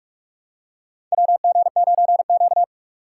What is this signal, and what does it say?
Morse code, heard as a keyed tone, about 45 wpm, 700 Hz. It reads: WG9Q
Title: USA Call Signs - 4 Characters - 4x Character Spacing - 45wpm